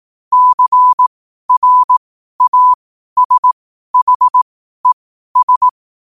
Morse code, heard as a keyed tone, about 18 wpm, spaced as for 12 wpm, 1000 Hz. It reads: CRASHES
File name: qotdfiles / 12wpm